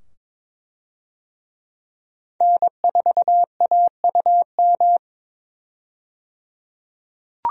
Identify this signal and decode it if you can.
Morse code, heard as a keyed tone, about 22 wpm, 700 Hz. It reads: N4AUM
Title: USA Call Signs - 5 Characters - 22wpm